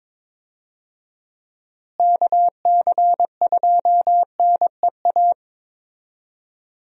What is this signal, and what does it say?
Morse code, heard as a keyed tone, about 22 wpm, 700 Hz. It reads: KC2NEA
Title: USA Call Signs - 6 Characters - Repeated 2x - 22wpm